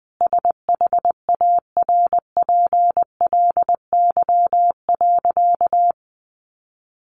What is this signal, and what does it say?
Morse code, heard as a keyed tone, about 20 wpm, 700 Hz. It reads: SHARPLY.
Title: A Sound Of Thunder 20wpm